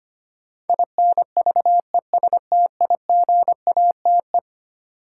Morse code, heard as a keyed tone, about 25 wpm, 700 Hz. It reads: INVESTIGATE
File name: practice / 25wpm / Eleven Letter Words - 25wpm